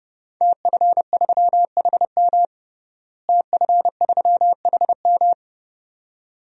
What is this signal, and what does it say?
Morse code, heard as a keyed tone, about 30 wpm, 700 Hz. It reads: TF3HM TF3HM
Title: DXCC-100 Call Signs - 5 Characters - Repeated 2x - 30wpm